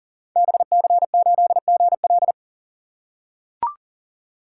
Morse code, heard as a keyed tone, about 41 wpm, 700 Hz. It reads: BC8GL
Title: Code Groups - 5 Letter:Numbers - 40wpm